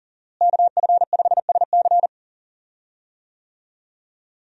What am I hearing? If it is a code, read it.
Morse code, heard as a keyed tone, about 40 wpm, 700 Hz. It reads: KF5SC
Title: USA Call Signs - 5 Characters - Repeated 2x - 40wpm